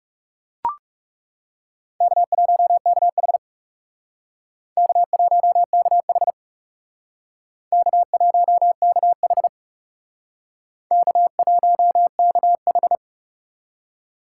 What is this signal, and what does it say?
Morse code, heard as a keyed tone, about 38 wpm, 700 Hz. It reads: K1KH K1KH K1KH K1KH